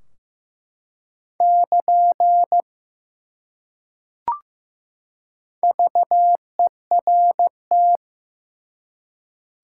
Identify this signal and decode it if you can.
Morse code, heard as a keyed tone, about 15 wpm, 700 Hz. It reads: <KN> VERT
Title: Top 180 QSO Elements 15wpm